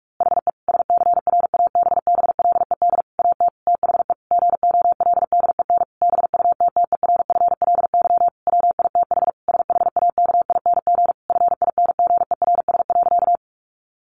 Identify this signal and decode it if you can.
Morse code, heard as a keyed tone, about 45 wpm, 700 Hz. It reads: HE SCRABBLED AT THE GOLDEN BUTTERFLY WITH SHAKING FINGERS.
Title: A Sound Of Thunder 45wpm